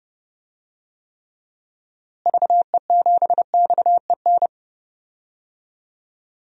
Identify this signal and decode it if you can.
Morse code, heard as a keyed tone, about 30 wpm, 700 Hz. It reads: VE7XEN
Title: DXCC-100 Call Signs - 6 Characters - Repeated 2x - 30wpm